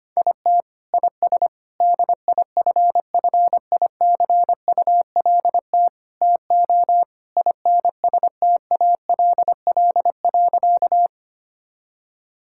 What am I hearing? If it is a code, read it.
Morse code, heard as a keyed tone, about 25 wpm, 700 Hz. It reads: IT IS DIFFICULT TO INSTALL.
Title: Sentences from Top 1600 Words 25wpm